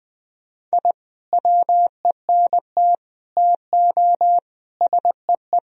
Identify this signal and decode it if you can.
Morse code, heard as a keyed tone, about 20 wpm, 700 Hz. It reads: I WENT TO SEE
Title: Sentences from Top 1000 Words - Review Part 1 - 20wpm